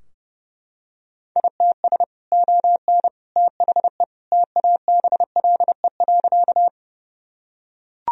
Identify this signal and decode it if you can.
Morse code, heard as a keyed tone, about 32 wpm, 700 Hz. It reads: ITS ON THE TABLE.
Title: Sentences from Top 1000 Words - Review Part 2 - 30wpm